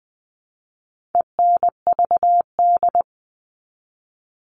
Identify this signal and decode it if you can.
Morse code, heard as a keyed tone, about 20 wpm, 700 Hz. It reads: ENVD
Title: All Cut Numbers - 4-Digit 20wpm